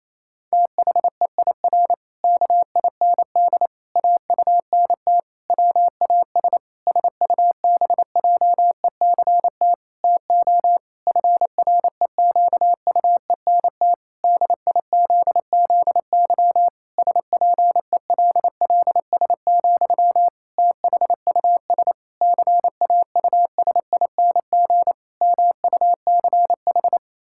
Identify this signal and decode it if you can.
Morse code, heard as a keyed tone, about 28 wpm, 700 Hz. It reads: THEIR KIND AUNT WAS SUBJECT TO FREQUENT DIZZY SPELLS, THUS CAUSING MUCH
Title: Pangram Challenge v2 - 28wpm